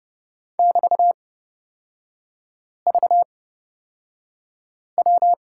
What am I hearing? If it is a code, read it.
Morse code, heard as a keyed tone, about 31 wpm, spaced as for 5 wpm, 700 Hz. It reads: <BT>VW
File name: Lesson 36e - BT - Three Characters 30wpm